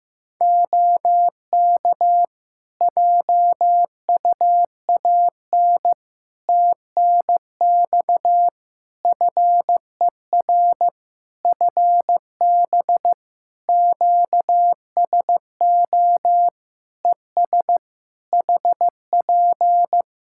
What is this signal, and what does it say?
Morse code, heard as a keyed tone, about 15 wpm, 700 Hz. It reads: OK JUAN TNX FER FB QSO ES HP
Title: K1USN SST QSO Practice 15wpm